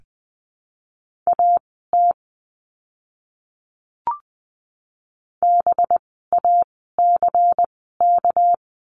Morse code, heard as a keyed tone, about 20 wpm, spaced as for 15 wpm, 700 Hz. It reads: AT BACK
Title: 230 Lesson 37f - Letter x - Words 20wpm